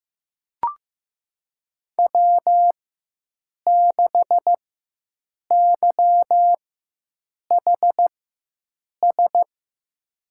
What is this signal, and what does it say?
Morse code, heard as a keyed tone, about 15 wpm, spaced as for 7 wpm, 700 Hz. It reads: W6YHS
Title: USA Call Signs - 5 Characters - 4x Character Spacing - 15wpm